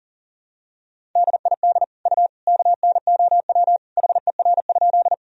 Morse code, heard as a keyed tone, about 40 wpm, 700 Hz. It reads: DID U KNOW HER?
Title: Sentences from Top 100 Words and Abbreviations 40wpm